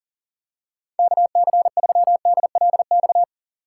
Morse code, heard as a keyed tone, about 38 wpm, 700 Hz. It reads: KC3DLX
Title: USA Call Signs - 6 Characters - 3x Character Spacing - 40wpm